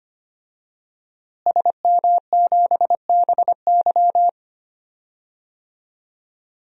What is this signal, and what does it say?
Morse code, heard as a keyed tone, about 25 wpm, 700 Hz. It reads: SM7BY
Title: DXCC-100 Call Signs - 5 Characters - Repeated 2x - 25wpm